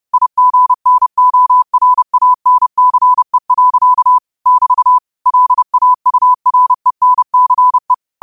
Morse code, heard as a keyed tone, about 30 wpm, 1000 Hz. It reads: IGNORANCE. <BT> LAURENCE
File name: qotdfiles / 30wpm